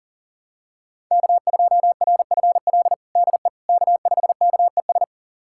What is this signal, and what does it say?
Morse code, heard as a keyed tone, about 40 wpm, 700 Hz. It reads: K2RFL DE K5KES
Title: USA Callsign DE Callsign - 5 Characters - 40wpm